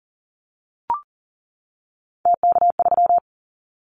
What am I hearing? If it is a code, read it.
Morse code, heard as a keyed tone, about 40 wpm, 700 Hz. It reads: TK3
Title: Callsign Prefixes for Top 100 DXCC - 40wpm